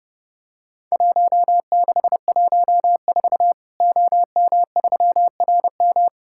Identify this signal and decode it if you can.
Morse code, heard as a keyed tone, about 30 wpm, 700 Hz. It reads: 1614 OM3RM
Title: Worked All Europe DX Contest - QTC Exchange - 30wpm